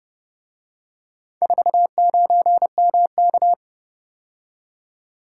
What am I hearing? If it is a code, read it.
Morse code, heard as a keyed tone, about 29 wpm, 700 Hz. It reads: 49MK